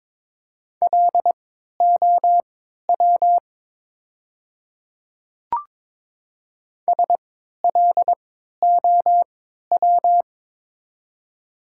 Morse code, heard as a keyed tone, about 22 wpm, spaced as for 12 wpm, 700 Hz. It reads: LOW SLOW
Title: 3x Character Spacing - Top 500 Words - 22wpm